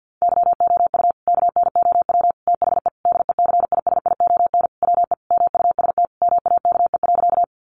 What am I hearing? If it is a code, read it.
Morse code, heard as a keyed tone, about 48 wpm, 700 Hz. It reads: YOU KNOW THE DECISION WE MUST MAKE.
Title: Sentences from Top 1000 Words 50wpm